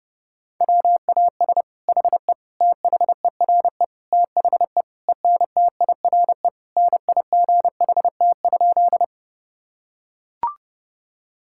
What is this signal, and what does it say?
Morse code, heard as a keyed tone, about 32 wpm, 700 Hz. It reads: WAS HE THERE THE ENTIRE NIGHT?
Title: Sentences from Top 1000 Words - Review Part 5 - 30wpm